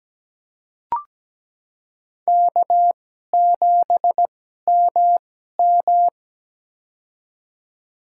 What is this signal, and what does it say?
Morse code, heard as a keyed tone, about 17 wpm, spaced as for 12 wpm, 700 Hz. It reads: K7MM
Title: USA Call Signs - 4 Characters - 2x Character Spacing - 17wpm